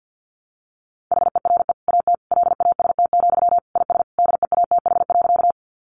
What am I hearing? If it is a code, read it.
Morse code, heard as a keyed tone, about 50 wpm, 700 Hz. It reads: HERE AT LAST, IS DEATH.
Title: Sentences from Top 1100 Words 50wpm